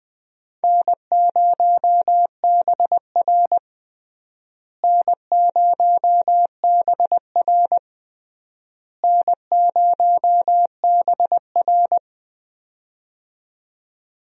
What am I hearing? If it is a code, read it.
Morse code, heard as a keyed tone, about 20 wpm, 700 Hz. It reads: N0BR N0BR N0BR
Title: USA Call Signs - 4 Characters - Repeated 3x - 20wpm